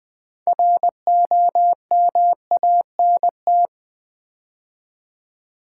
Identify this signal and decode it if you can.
Morse code, heard as a keyed tone, about 20 wpm, 700 Hz. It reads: ROMANT
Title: Build A Word - 3 Letters At A Time - 8-Letter Words - 20wpm